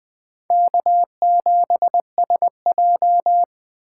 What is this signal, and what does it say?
Morse code, heard as a keyed tone, about 21 wpm, 700 Hz. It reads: K7SJ
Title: USA Call Signs - 3x Character Spacing - 20wpm